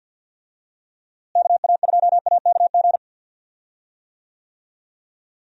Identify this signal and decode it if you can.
Morse code, heard as a keyed tone, about 47 wpm, 700 Hz. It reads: KA2AKG
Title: USA Call Signs - 6 Characters - Repeated 3x - 50wpm